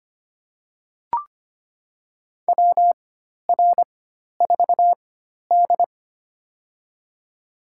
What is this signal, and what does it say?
Morse code, heard as a keyed tone, about 25 wpm, spaced as for 12 wpm, 700 Hz. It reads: WR4D
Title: USA Call Signs - 4 Characters - 4x Character Spacing - 25wpm